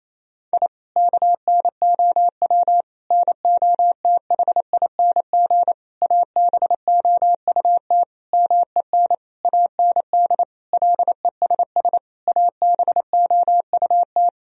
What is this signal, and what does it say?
Morse code, heard as a keyed tone, about 28 wpm, 700 Hz. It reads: I KNOW NOTHING ABOUT MEN AND LESS ABOUT
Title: Sentences from Top 1000 Words - Review Part 2 - 28wpm